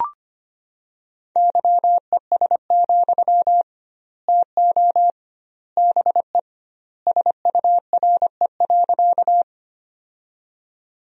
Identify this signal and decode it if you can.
Morse code, heard as a keyed tone, about 25 wpm, 700 Hz. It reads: YES, TO BE SURE.
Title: 2x Word-Spacing - Sentences from Top 500 Words - 25wpm